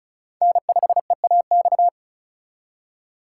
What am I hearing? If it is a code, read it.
Morse code, heard as a keyed tone, about 43 wpm, 700 Hz. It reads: N5EAX
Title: Speed Racing - USA 1x3 Call Signs 35wpm